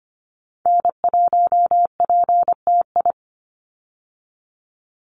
Morse code, heard as a keyed tone, about 24 wpm, 700 Hz. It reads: N1PTI
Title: USA Call Signs 25wpm